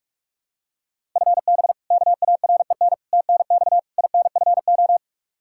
Morse code, heard as a keyed tone, about 41 wpm, 700 Hz. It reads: FB KAREN TNX INFO
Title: K1USN SST QSO Practice 45wpm